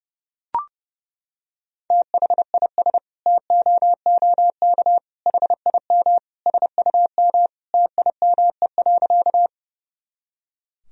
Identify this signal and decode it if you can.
Morse code, heard as a keyed tone, about 31 wpm, 700 Hz. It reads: THIS TOOK HIM SUM TIME.
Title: Sentences from Top 300 Words and Abbreviations 30wpm